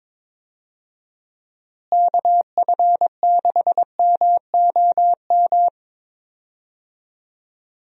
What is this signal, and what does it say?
Morse code, heard as a keyed tone, about 22 wpm, 700 Hz. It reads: KF6MOM
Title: USA Call Signs - 6 Characters - Repeated 3x - 22wpm